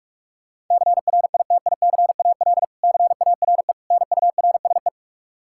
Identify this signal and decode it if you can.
Morse code, heard as a keyed tone, about 41 wpm, 700 Hz. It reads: CRITICAL CARE NURSE
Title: Practice QSO - Exchange 3 - More Info - 45wpm